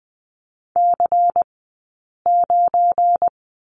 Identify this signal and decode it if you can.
Morse code, heard as a keyed tone, about 20 wpm, spaced as for 8 wpm, 700 Hz. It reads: C9
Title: 177 Lesson 29d - Slash - Two Characters 20wpm